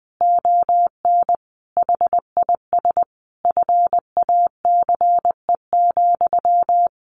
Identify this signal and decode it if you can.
Morse code, heard as a keyed tone, about 20 wpm, 700 Hz. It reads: ON HIS FACE,
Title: A Sound Of Thunder 20wpm